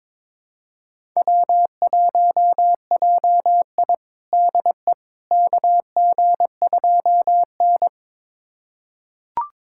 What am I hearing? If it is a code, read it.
Morse code, heard as a keyed tone, about 22 wpm, 700 Hz. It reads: W1JI DE KG2N
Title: USA Callsign DE Callsign - 4 Characters - 22wpm